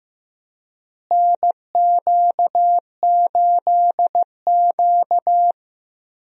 Morse code, heard as a keyed tone, about 15 wpm, 700 Hz. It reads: NQ8Q
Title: USA Call Signs - 4 Characters - Repeated 3x - 15wpm